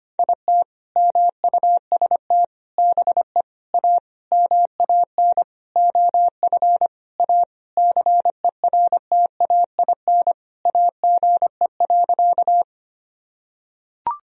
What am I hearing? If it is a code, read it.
Morse code, heard as a keyed tone, about 25 wpm, 700 Hz. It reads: IT MUST BE A MAN OF A CERTAIN AGE.
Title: Sentences from Top 1000 Words - Review Part 3 - 25wpm